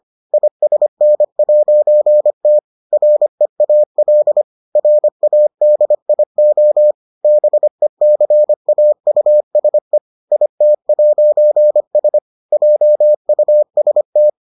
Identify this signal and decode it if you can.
Morse code, heard as a keyed tone, about 25 wpm, 600 Hz. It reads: ISN'T REAL RADIO BECAUSE IT'S JUST